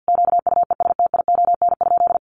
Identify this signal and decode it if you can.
Morse code, heard as a keyed tone, about 50 wpm, 700 Hz. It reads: QUESTION?
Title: Sentences from Top 700 Words 50wpm